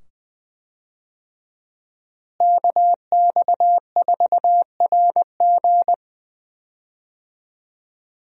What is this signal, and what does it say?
Morse code, heard as a keyed tone, about 21 wpm, 700 Hz. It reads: KX4RG
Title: USA Call Signs - 5 Characters - Repeated 2x - 20wpm